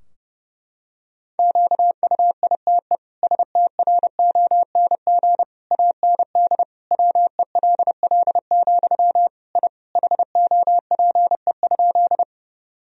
Morse code, heard as a keyed tone, about 30 wpm, 700 Hz. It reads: QUITE STRONG AND WELL, I HOPE?
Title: Sentences from Top 1000 Words - Review Part 3 - 30wpm